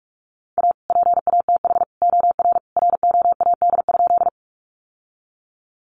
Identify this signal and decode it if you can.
Morse code, heard as a keyed tone, about 44 wpm, 700 Hz. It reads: A PATH OR ROAD?
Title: Sentences from Top 1400 Words 45wpm